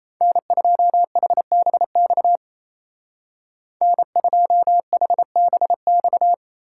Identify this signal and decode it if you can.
Morse code, heard as a keyed tone, about 34 wpm, 700 Hz. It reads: N2HBX N2HBX
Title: Speed Racing - USA 1x3 Call Signs 28wpm